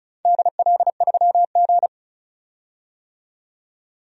Morse code, heard as a keyed tone, about 34 wpm, 700 Hz. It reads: DL3G
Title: DXCC-100 Call Signs - 4 Characters - 35wpm